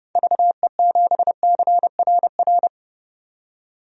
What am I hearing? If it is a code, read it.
Morse code, heard as a keyed tone, about 30 wpm, 700 Hz. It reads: VE7CRR
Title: DXCC-100 Call Signs - 6 Characters - 30wpm